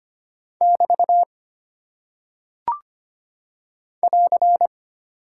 Morse code, heard as a keyed tone, about 25 wpm, 700 Hz. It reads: <BT> <AR>
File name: practice / 25wpm / Single Punctuation-Prosign - 25wpm